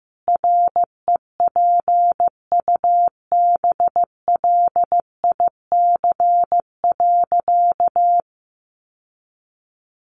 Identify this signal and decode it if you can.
Morse code, heard as a keyed tone, about 15 wpm, 700 Hz. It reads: REPUBLIC.
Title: Sentences from Top 1500 Words 15wpm